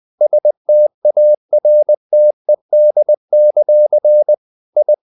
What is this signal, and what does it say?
Morse code, heard as a keyed tone, about 20 wpm, 600 Hz. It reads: STARTED; I
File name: the old man and the sea part 2 20wpm